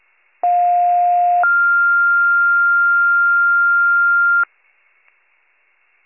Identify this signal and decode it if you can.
Two-tone page, heard as a one-second tone, then a three-second tone, about 710 then 1400 Hz. It is heard from 0.4 to 4.4 s.